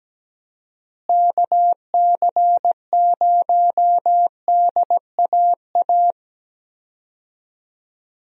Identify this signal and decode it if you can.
Morse code, heard as a keyed tone, about 17 wpm, 700 Hz. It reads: KC0DAA
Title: USA Call Signs - 6 Characters - Repeated 3x - 17wpm